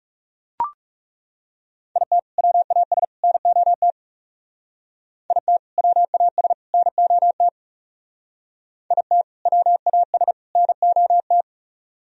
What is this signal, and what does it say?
Morse code, heard as a keyed tone, about 36 wpm, 700 Hz. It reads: IT WAS NOT IT WAS NOT IT WAS NOT
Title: Speed-Racing - Sets of 3 Words 30wpm